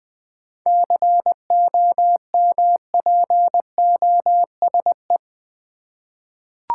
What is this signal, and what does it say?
Morse code, heard as a keyed tone, about 20 wpm, 690 Hz. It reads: COMPOSE
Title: 7 Letter Words - First 3 Letters Given - 20wpm